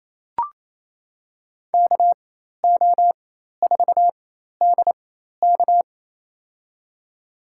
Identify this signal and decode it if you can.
Morse code, heard as a keyed tone, about 28 wpm, spaced as for 13 wpm, 700 Hz. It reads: KO4DK
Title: USA Call Signs - 5 Characters - 4x Character Spacing - 28wpm